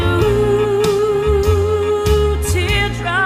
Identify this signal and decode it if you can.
Congestion tone (425 Hz), with noise.